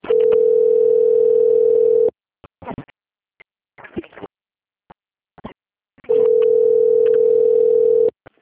Ringback tone (North America), 440 + 480 Hz.
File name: out- - - - - .11526